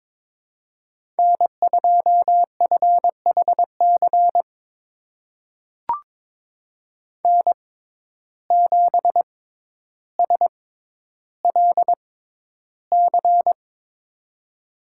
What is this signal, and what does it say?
Morse code, heard as a keyed tone, about 22 wpm, spaced as for 11 wpm, 700 Hz. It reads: N2FHC N7SLC
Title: USA Call Signs - 5 Characters - 6x Character Spacing - 22wpm